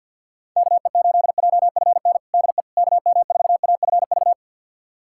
Morse code, heard as a keyed tone, about 47 wpm, 700 Hz. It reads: KE8JFN DE KM4AFV
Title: USA Callsign DE Callsign - 6 Characters - 50wpm